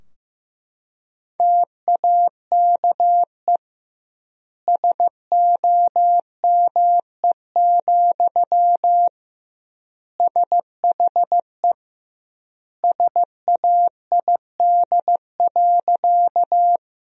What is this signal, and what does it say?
Morse code, heard as a keyed tone, about 15 wpm, 700 Hz. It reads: TAKE SOME, SHE SAID.
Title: 2x Word-Spacing - Sentences from Top 200 Words - 15wpm